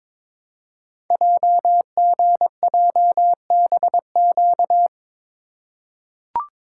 Morse code, heard as a keyed tone, about 22 wpm, 700 Hz. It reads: JGJBQ
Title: Code Groups - 5 Letters - 22wpm